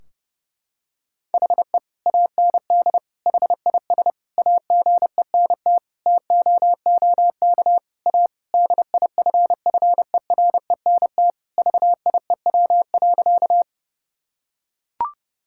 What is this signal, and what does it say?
Morse code, heard as a keyed tone, about 32 wpm, 700 Hz. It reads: HE AND HIS AGENT TOOK A DIFFERENT VIEW.
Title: Sentences from Top 1600 Words 30wpm